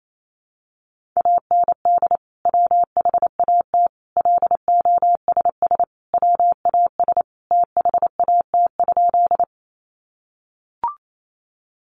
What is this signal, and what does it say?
Morse code, heard as a keyed tone, about 28 wpm, 700 Hz. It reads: AND WHAT LOSS WAS THAT?
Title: Sentences from Top 1400 Words 28wpm